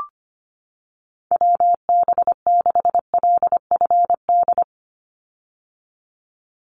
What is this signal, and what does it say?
Morse code, heard as a keyed tone, about 25 wpm, 700 Hz. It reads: WB6LFD
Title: USA Call Signs 25wpm